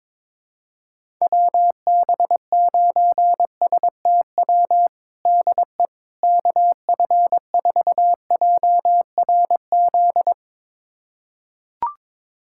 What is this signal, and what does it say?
Morse code, heard as a keyed tone, about 22 wpm, 700 Hz. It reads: WB9STW DE KF4JRZ
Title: USA Callsign DE Callsign 22wpm